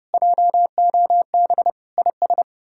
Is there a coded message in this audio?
Yes: Morse code from 0.1 to 2.4 s, heard as a keyed tone, about 40 wpm, 700 Hz. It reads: JOB IS